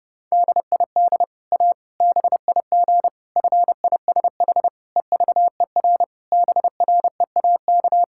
Morse code, heard as a keyed tone, about 30 wpm, 700 Hz. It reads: DID A BIG FISH EVER BREAK
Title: Sentences from Top 1000 Words - Review Part 6 - 30wpm